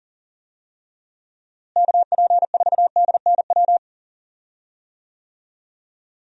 Morse code, heard as a keyed tone, about 38 wpm, 700 Hz. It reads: KP4DNW
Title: USA Call Signs - 6 Characters - Repeated 3x - 40wpm